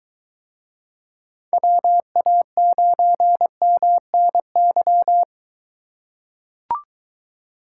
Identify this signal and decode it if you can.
Morse code, heard as a keyed tone, about 23 wpm, 700 Hz. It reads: WA9MNY